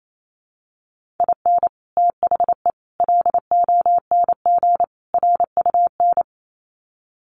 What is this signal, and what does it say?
Morse code, heard as a keyed tone, about 28 wpm, 700 Hz. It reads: IN THE LONG RUN
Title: Sets of 4 Words 28wpm